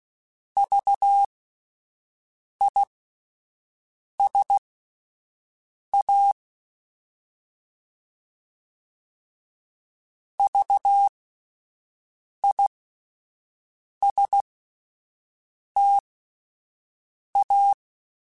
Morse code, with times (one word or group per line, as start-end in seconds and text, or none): 0.6-6.3 VISA
10.4-17.7 VISTA